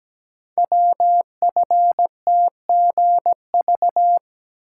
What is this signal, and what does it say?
Morse code, heard as a keyed tone, about 17 wpm, 700 Hz. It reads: WFTGV
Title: Code Groups - 5 Letters - 17wpm